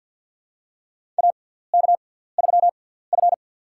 Morse code, heard as a keyed tone, about 53 wpm, spaced as for 17 wpm, 700 Hz. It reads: AK3F
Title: USA Call Signs - 6x Character Spacing - 50wpm